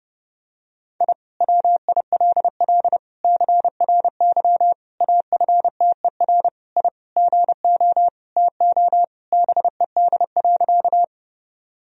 Morse code, heard as a keyed tone, about 30 wpm, 700 Hz. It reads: I WILL CRY AFTER I GO TO BED.
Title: Sentences from Top 1000 Words - Review Part 2 - 30wpm